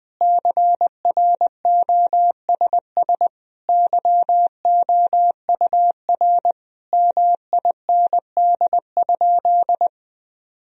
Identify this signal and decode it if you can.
Morse code, heard as a keyed tone, about 20 wpm, 700 Hz. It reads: CROSS YOUR MIND?
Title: Sentences from Top 1000 Words - Review Part 1 - 20wpm